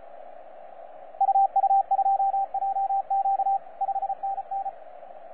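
Morse code, heard as a keyed tone, about 33 wpm, 780 Hz, with radio noise. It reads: UU2JQ 5NN
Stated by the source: FS/LJ2IJ heard on 12 metres CW, from DX World.net.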